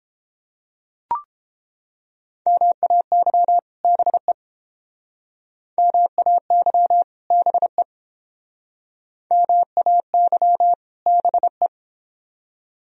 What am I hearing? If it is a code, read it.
Morse code, heard as a keyed tone, about 30 wpm, 700 Hz. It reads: MAY BE MAY BE MAY BE